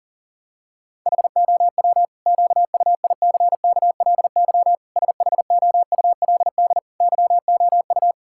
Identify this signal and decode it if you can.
Morse code, heard as a keyed tone, about 40 wpm, 700 Hz. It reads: HOW QUICKLY SHOULD YOU
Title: Sentences from Top 1000 Words - Review Part 5 - 40wpm